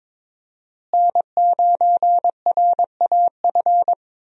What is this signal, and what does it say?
Morse code, heard as a keyed tone, about 22 wpm, 700 Hz. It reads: N9RAF
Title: USA Call Signs - 5 Characters - Repeated 2x - 22wpm